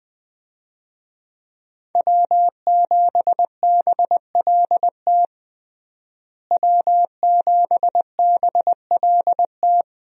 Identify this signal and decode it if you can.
Morse code, heard as a keyed tone, about 20 wpm, 700 Hz. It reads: W7BLT W7BLT
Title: USA Call Signs - 5 Characters - Repeated 2x - 20wpm